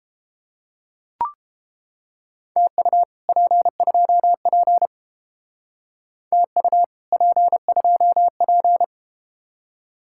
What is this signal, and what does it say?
Morse code, heard as a keyed tone, about 31 wpm, 700 Hz. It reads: TU P2P TU P2P